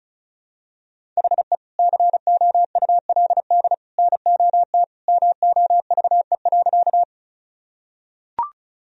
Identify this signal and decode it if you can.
Morse code, heard as a keyed tone, about 34 wpm, 700 Hz. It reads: HE COULD NOT MOVE.
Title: Sentences from Top 1000 Words - Review Part 1 - 35wpm